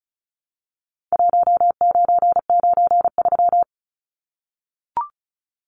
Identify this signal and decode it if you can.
Morse code, heard as a keyed tone, about 34 wpm, 700 Hz. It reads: 1993
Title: Numbers - Contest Style - 4-Digit 35wpm